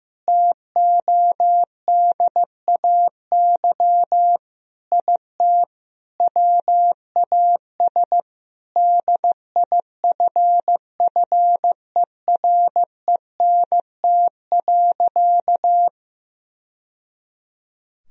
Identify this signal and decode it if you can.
Morse code, heard as a keyed tone, about 15 wpm, 700 Hz. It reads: TODAY IT WAS DIFFERENT.
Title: Sentences from Top 1000 Words - Review Part 4 - 15wpm